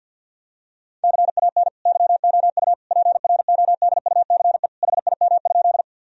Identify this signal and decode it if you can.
Morse code, heard as a keyed tone, about 50 wpm, 700 Hz. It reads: CAN YOU PRODUCE HIM?
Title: Sentences from Top 1000 Words - Review Part 3 - 50wpm